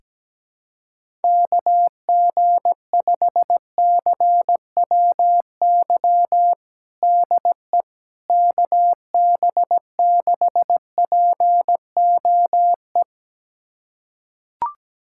Morse code, heard as a keyed tone, about 17 wpm, 700 Hz. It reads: KG5CWY DE KB6POE
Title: USA Callsign DE Callsign - 6 Characters - 17wpm